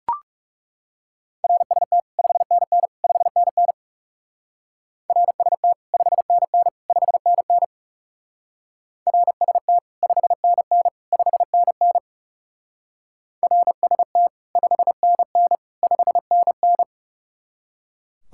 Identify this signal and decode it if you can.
Morse code, heard as a keyed tone, about 38 wpm, 700 Hz. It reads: RST 5NN 5NN RST 5NN 5NN RST 5NN 5NN RST 5NN 5NN